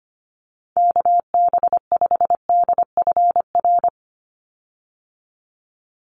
Morse code, heard as a keyed tone, about 25 wpm, 700 Hz. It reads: KB5DFR
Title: USA Call Signs 25wpm